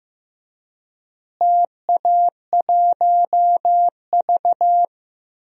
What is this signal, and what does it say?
Morse code, heard as a keyed tone, about 15 wpm, 700 Hz. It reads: TA1V
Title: DXCC-100 Call Signs - 4 Characters - Repeated 3x - 15wpm